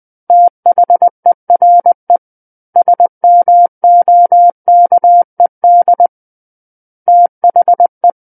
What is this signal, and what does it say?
Morse code, heard as a keyed tone, about 20 wpm, 700 Hz. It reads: THERE SMOKED THE
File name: 20 WPM